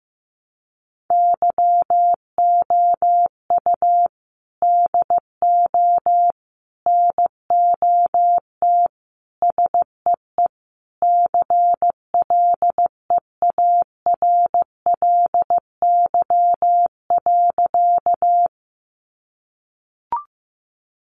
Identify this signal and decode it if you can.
Morse code, heard as a keyed tone, about 15 wpm, 700 Hz. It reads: YOU DO NOT SEE CLEARLY.
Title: Sentences from Top 1300 Words 15wpm